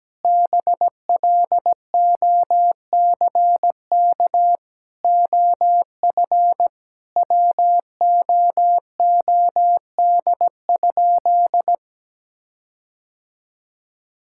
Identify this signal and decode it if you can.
Morse code, heard as a keyed tone, about 17 wpm, 700 Hz. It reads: BLOCK OF WOOD?
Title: Sentences from Top 1600 Words 17wpm